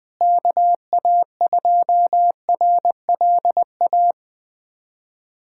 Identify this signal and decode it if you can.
Morse code, heard as a keyed tone, about 20 wpm, 700 Hz. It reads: KA2RLA
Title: USA Callsign DE Callsign 20wpm